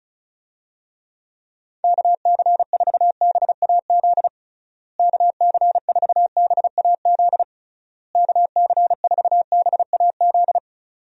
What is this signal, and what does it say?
Morse code, heard as a keyed tone, about 36 wpm, 700 Hz. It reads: KC4BAZ KC4BAZ KC4BAZ
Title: USA Call Signs - 6 Characters - Repeated 3x - 35wpm